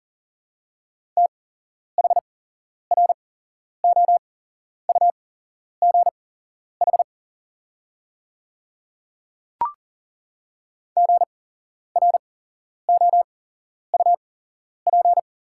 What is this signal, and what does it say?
Morse code, heard as a keyed tone, about 40 wpm, spaced as for 11 wpm, 700 Hz. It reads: THROUGH GROUP
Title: 8x Character Spacing - Top 300 Words - 40wpm